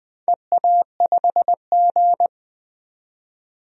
Morse code, heard as a keyed tone, about 20 wpm, 700 Hz. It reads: EA5G
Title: DXCC-100 Call Signs - 4 Characters - 20wpm